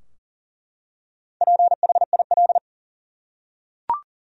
Morse code, heard as a keyed tone, about 40 wpm, 700 Hz. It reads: PHIL
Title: USA Common Names - Max 5 Letters - 40wpm